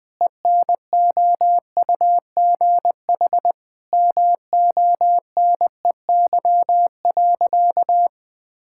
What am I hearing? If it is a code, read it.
Morse code, heard as a keyed tone, about 21 wpm, 700 Hz. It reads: ENOUGH MONEY.
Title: Sentences with 4-Word Phrases - 20wpm